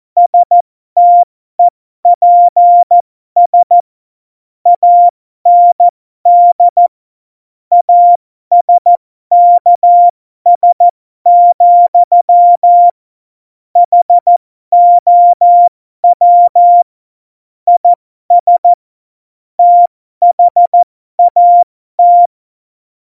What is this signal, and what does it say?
Morse code, heard as a keyed tone, about 14 wpm, spaced as for 12 wpm, 700 Hz. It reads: STEPS AND ASKS, HOW IS THAT
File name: Morsekurs / Witze